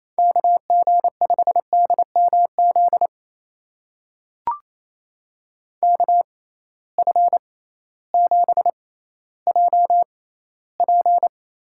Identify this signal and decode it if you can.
Morse code, heard as a keyed tone, about 28 wpm, spaced as for 15 wpm, 700 Hz. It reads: KG5DMZ KF7JP
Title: USA Call Signs - 6 Characters - 6x Character Spacing - 28wpm